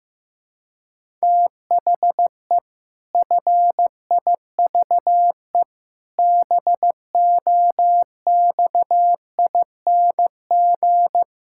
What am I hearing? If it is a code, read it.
Morse code, heard as a keyed tone, about 15 wpm, 700 Hz. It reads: THE FIVE BOXING
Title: Pangram Challenge v2 - 15wpm